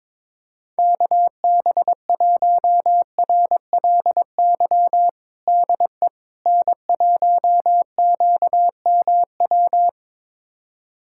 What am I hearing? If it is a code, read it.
Morse code, heard as a keyed tone, about 22 wpm, 700 Hz. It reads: KB1RLY DE N1QMW
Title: USA Callsign DE Callsign 22wpm